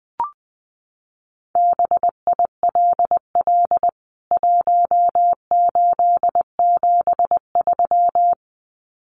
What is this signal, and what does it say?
Morse code, heard as a keyed tone, about 20 wpm, 700 Hz. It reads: BILL 1873
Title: CWT Exchange 20wpm